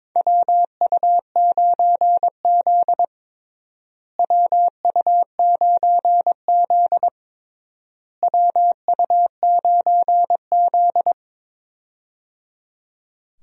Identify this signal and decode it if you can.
Morse code, heard as a keyed tone, about 22 wpm, 700 Hz. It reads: WU9Z WU9Z WU9Z
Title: USA Call Signs - 4 Characters - Repeated 3x - 22wpm